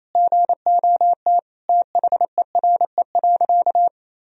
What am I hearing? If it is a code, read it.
Morse code, heard as a keyed tone, about 28 wpm, 700 Hz. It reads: GOT THERE.
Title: Sentences from Top 1000 Words - Review Part 1 - 28wpm